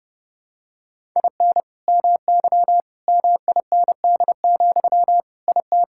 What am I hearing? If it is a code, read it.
Morse code, heard as a keyed tone, about 32 wpm, 700 Hz. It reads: IN MY MIND, IT
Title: Sentences from Top 1000 Words - Review Part 6 - 30wpm